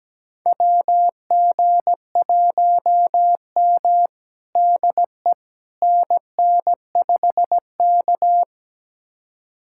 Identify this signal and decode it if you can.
Morse code, heard as a keyed tone, about 17 wpm, 700 Hz. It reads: WG1M DE NN5K
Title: USA Callsign DE Callsign - 4 Characters - 17wpm